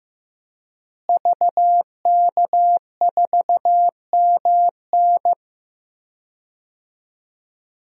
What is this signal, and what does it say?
Morse code, heard as a keyed tone, about 16 wpm, 700 Hz. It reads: VK4MN
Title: DXCC-100 Call Signs - 5 Characters - Repeated 2x - 15wpm